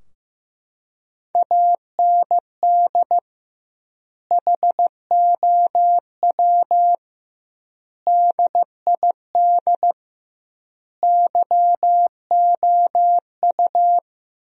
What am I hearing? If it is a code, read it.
Morse code, heard as a keyed tone, about 15 wpm, 700 Hz. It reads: AND HOW DID YOU
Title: 2x Word-Spacing - Sentences from Top 200 Words - 15wpm